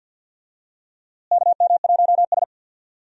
Morse code, heard as a keyed tone, about 52 wpm, 700 Hz. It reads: KM1S
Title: USA Call Signs - 4 Characters - Repeated 3x - 50wpm